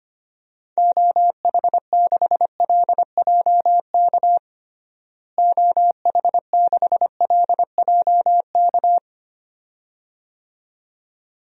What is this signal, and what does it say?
Morse code, heard as a keyed tone, about 25 wpm, 700 Hz. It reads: OH6LJK OH6LJK
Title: DXCC-100 Call Signs - 6 Characters - Repeated 2x - 25wpm